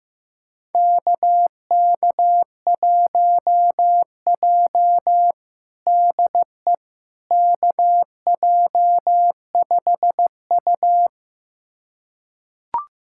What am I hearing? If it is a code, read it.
Morse code, heard as a keyed tone, about 15 wpm, 700 Hz. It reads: KK1J DE KJ5U
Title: USA Callsign DE Callsign - 4 Characters - 15wpm